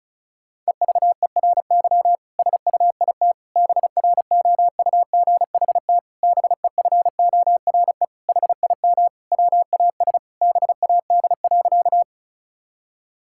Morse code, heard as a keyed tone, about 36 wpm, 700 Hz. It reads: EVERY SUIT BROUGHT BEFORE HIM WAS BAD.
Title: Sentences from Top 1600 Words 35wpm